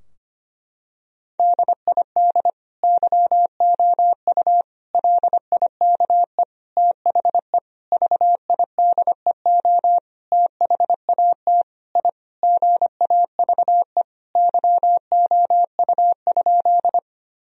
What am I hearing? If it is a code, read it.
Morse code, heard as a keyed tone, about 25 wpm, 700 Hz. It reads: DID YOU LIKE THE VIDEO THAT I GAVE YOU?
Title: Sentences from Top 1000 Words - Review Part 4 - 25wpm